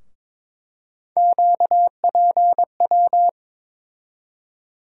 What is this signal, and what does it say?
Morse code, heard as a keyed tone, about 22 wpm, 700 Hz. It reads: QPW
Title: USA Call Sign Postfixes - 3 Letters 22wpm